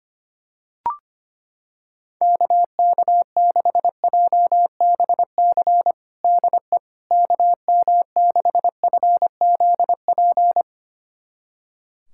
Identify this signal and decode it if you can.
Morse code, heard as a keyed tone, about 25 wpm, 700 Hz. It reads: KK6JBC DE KM6FZP
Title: USA Callsign DE Callsign - 6 Characters - 25wpm